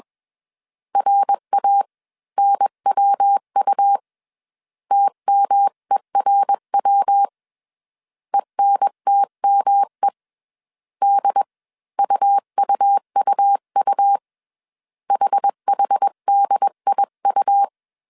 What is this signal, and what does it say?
Morse code, heard as a keyed tone, about 21 wpm, spaced as for 19 wpm, 800 Hz. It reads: RANWU TMERW ENTME DUUUU HHDIU